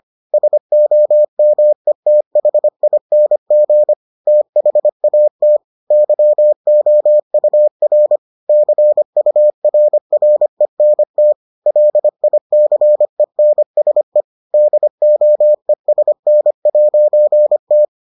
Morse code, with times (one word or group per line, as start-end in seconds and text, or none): 0.3-3.9 SOMETHING
4.3-5.6 THAT
5.9-8.2 YOUR
8.5-11.3 CURRENT
11.7-14.2 LICENSE
14.5-17.9 DOESN'T